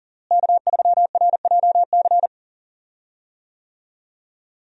Morse code, heard as a keyed tone, about 37 wpm, 700 Hz. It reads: K3RJC
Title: USA Call Signs - 5 Characters - Repeated 2x - 40wpm